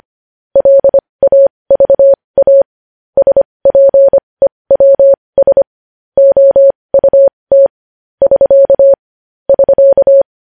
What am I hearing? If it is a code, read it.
Morse code, heard as a keyed tone, about 25 wpm, spaced as for 20 wpm, 550 Hz. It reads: LAVA SPEWS OUT <SK> <SK>
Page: news-headlines-in-morse-code-at-20-wpm